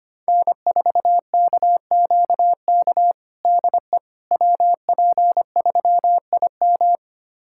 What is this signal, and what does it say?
Morse code, heard as a keyed tone, about 25 wpm, 700 Hz. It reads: N4KQK DE WP3IM
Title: USA Callsign DE Callsign - 5 Characters - 25wpm